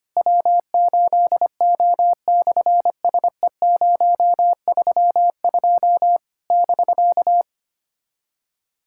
Morse code, heard as a keyed tone, about 25 wpm, 700 Hz. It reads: W8O/SE032 <BK>
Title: SOTA QSO Practice 25wpm